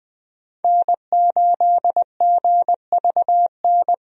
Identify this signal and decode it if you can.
Morse code, heard as a keyed tone, about 20 wpm, 700 Hz. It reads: N8GVN
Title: USA Call Signs - 5 Characters - 20wpm